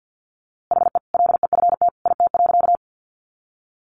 Morse code, heard as a keyed tone, about 50 wpm, 700 Hz. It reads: HE LEFT IT.
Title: A Sound Of Thunder 50wpm